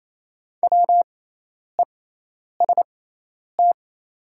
Morse code, heard as a keyed tone, about 28 wpm, spaced as for 10 wpm, 700 Hz. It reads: WEST
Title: 6x Character Spacing - Top 500 Words - 28wpm